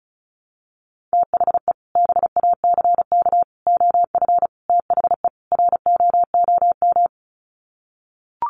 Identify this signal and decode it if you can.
Morse code, heard as a keyed tone, about 34 wpm, 700 Hz. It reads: THE BACK OF THE ROOM
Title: Sets of 5 Words Encore 35wpm